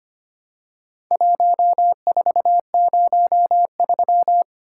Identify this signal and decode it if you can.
Morse code, heard as a keyed tone, about 25 wpm, 700 Hz. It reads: 1403
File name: practice / 25wpm / Four Letter-Number Combinations - 25wpm